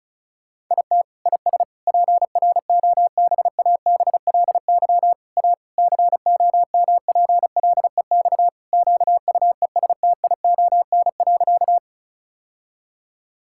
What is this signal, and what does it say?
Morse code, heard as a keyed tone, about 36 wpm, 700 Hz. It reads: IT IS PROBABLY A COMPLEX QUESTION.
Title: Sentences from Top 1000 Words - Review Part 6 - 35wpm